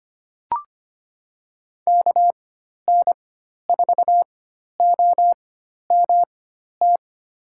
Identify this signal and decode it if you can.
Morse code, heard as a keyed tone, about 25 wpm, spaced as for 12 wpm, 700 Hz. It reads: KN4OMT
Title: USA Call Signs - 6 Characters - 4x Character Spacing - 25wpm